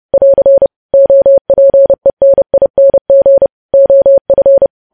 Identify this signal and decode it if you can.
Morse code, heard as a keyed tone, about 30 wpm, 550 Hz. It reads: <AR> OPENING OF
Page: news headlines in morse code at 30 wpm